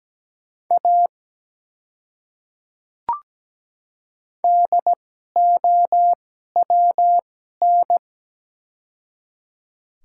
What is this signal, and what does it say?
Morse code, heard as a keyed tone, about 17 wpm, spaced as for 12 wpm, 700 Hz. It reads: A DOWN
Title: 2x Character Spacing - Top 200 Words - 17wpm